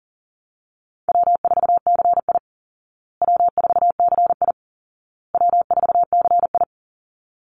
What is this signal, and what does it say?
Morse code, heard as a keyed tone, about 40 wpm, 700 Hz. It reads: W4CI W4CI W4CI
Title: Easy Call Signs Repeated 3x - 40wpm